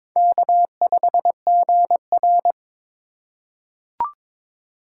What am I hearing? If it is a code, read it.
Morse code, heard as a keyed tone, about 22 wpm, 700 Hz. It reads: K5GR
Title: USA Call Signs - 3x Character Spacing - 22wpm